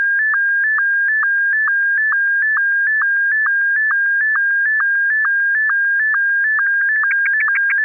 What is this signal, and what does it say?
SSTV (slow-scan television), Martin 1.
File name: 16.SSTV-MARTIN1